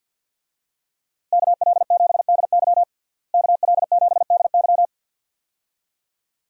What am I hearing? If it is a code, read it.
Morse code, heard as a keyed tone, about 51 wpm, 700 Hz. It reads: KL7DY KL7DY
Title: DXCC-100 Call Signs - 5 Characters - Repeated 2x - 50wpm